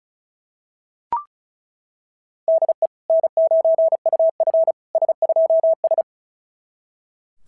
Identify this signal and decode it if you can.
Morse code, heard as a keyed tone, about 36 wpm, 690 Hz. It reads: DE N9UF S2S
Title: SOTA QSO Practice 35wpm